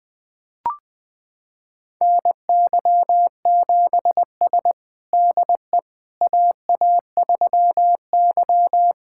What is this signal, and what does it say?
Morse code, heard as a keyed tone, about 20 wpm, 700 Hz. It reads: NY7S DE AA3Y
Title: USA Callsign DE Callsign - 4 Characters - 20wpm